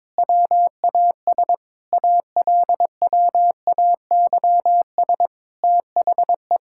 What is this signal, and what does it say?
Morse code, heard as a keyed tone, about 22 wpm, 700 Hz. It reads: WAS ALWAYS THE
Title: Sentences from Top 1000 Words - Review Part 5 - 22wpm